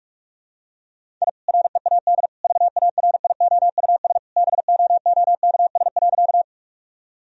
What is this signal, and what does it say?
Morse code, heard as a keyed tone, about 44 wpm, 700 Hz. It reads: I READ VARIOUS BOOKS.
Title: Sentences from Top 1000 Words - Review Part 4 - 45wpm